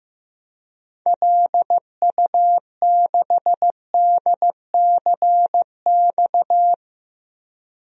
Morse code, heard as a keyed tone, about 15 wpm, 700 Hz. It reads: LU6DCX
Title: DXCC-100 Call Signs - 6 Characters - 15wpm